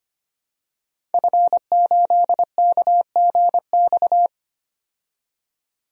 Morse code, heard as a keyed tone, about 25 wpm, 700 Hz. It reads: F8KGX
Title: DXCC-100 Call Signs - 5 Characters - 25wpm